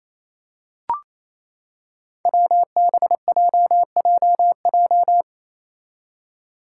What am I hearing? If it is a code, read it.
Morse code, heard as a keyed tone, about 27 wpm, 700 Hz. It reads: WBJJJ
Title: Five Challenging Letter-Number Combinations 28wpm